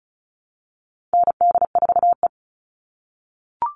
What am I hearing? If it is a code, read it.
Morse code, heard as a keyed tone, about 34 wpm, 700 Hz. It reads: ND4E
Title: All Cut Numbers - 4-Digit 35wpm